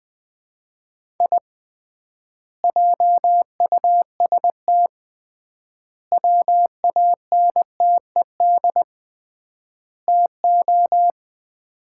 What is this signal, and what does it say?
Morse code, heard as a keyed tone, about 21 wpm, 700 Hz. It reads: I JUST WANTED TO
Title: 3x Word-Spacing - Sets of 4 Words - 20wpm